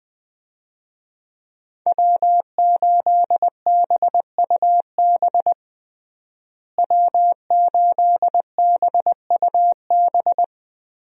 Morse code, heard as a keyed tone, about 20 wpm, 700 Hz. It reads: W8BUB W8BUB
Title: USA Call Signs - 5 Characters - Repeated 2x - 20wpm